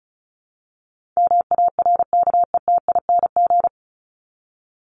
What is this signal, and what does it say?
Morse code, heard as a keyed tone, about 34 wpm, 700 Hz. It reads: MARKETING
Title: Top 751-1000 Words 35wpm